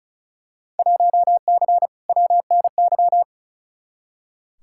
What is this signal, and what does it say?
Morse code, heard as a keyed tone, about 36 wpm, 700 Hz. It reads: 1C WNY
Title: Field Day Exchanges Encore - 35wpm